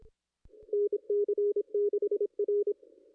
Morse code, heard as a keyed tone, about 28 wpm, 410 Hz. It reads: NC6R
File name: contests / ss-cw-k5tr-op-n5rz -2200